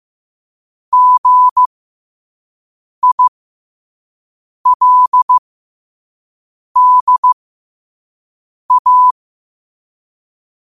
Morse code, heard as a keyed tone, about 15 wpm, spaced as for 5 wpm, 1000 Hz. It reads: GILDA